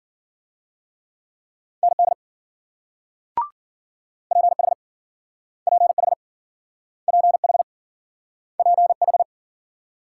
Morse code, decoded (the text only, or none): ND PH PH PH PH